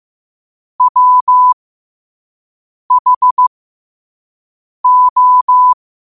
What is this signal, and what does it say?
Morse code, heard as a keyed tone, about 15 wpm, spaced as for 5 wpm, 1000 Hz. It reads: WHO
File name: qotdfiles / 5wpm